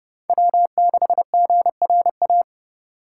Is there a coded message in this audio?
Yes: Morse code from 0.3 to 2.4 s, heard as a keyed tone, about 28 wpm, 700 Hz. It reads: W6GRA